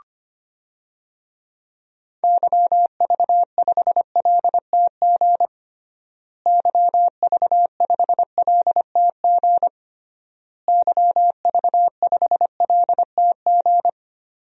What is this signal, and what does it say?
Morse code, heard as a keyed tone, about 25 wpm, 700 Hz. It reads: YV5LTG YV5LTG YV5LTG
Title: DXCC-100 Call Signs - 6 Characters - Repeated 3x - 25wpm